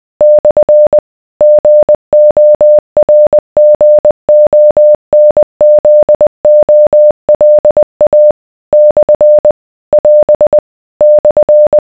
Morse code, heard as a keyed tone, about 20 wpm, 600 Hz. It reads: / GORGONZOLA / <AS> /